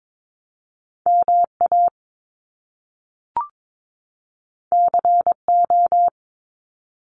Morse code, heard as a keyed tone, about 22 wpm, 700 Hz. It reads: MA CO
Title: US States 22wpm